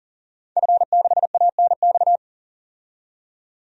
Morse code, heard as a keyed tone, about 40 wpm, 700 Hz. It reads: F6ANX
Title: DXCC-100 Call Signs - 5 Characters - 40wpm